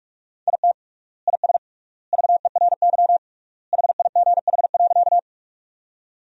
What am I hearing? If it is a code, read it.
Morse code, heard as a keyed tone, about 44 wpm, 700 Hz. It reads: IT IS VERY HIGH.
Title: 3x Word-Spacing - Sentences from Top 200 Words - 45wpm